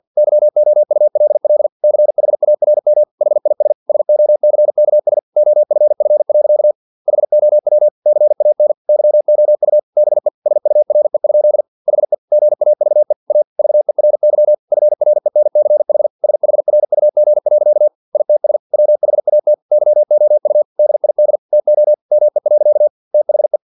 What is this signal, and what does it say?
Morse code, heard as a keyed tone, about 50 wpm, 600 Hz. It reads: YOULL CHARM HIS SOCKS OFF. HOW CAN YOU BE SURE? HE GAVE A VERY FRENCH SHRUG. ITS WHAT YOU DID TO ME. THE
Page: Never 49wpm